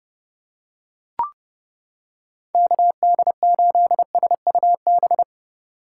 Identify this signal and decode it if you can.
Morse code, heard as a keyed tone, about 29 wpm, 700 Hz. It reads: KD8SUB